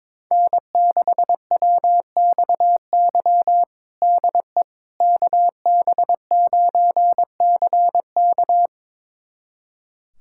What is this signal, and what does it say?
Morse code, heard as a keyed tone, about 22 wpm, 700 Hz. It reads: N6WXY DE KB9CK
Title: USA Callsign DE Callsign - 5 Characters - 22wpm